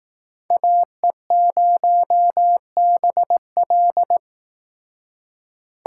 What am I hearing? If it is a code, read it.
Morse code, heard as a keyed tone, about 18 wpm, 700 Hz. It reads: AE0BL
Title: Speed Racing - USA 2x2 Call Signs 15wpm